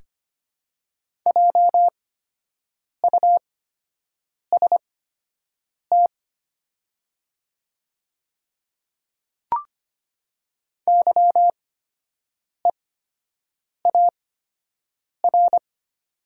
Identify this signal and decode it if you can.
Morse code, heard as a keyed tone, about 25 wpm, spaced as for 7 wpm, 700 Hz. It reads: JUST YEAR
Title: 8x Character Spacing - Top 200 Words - 25wpm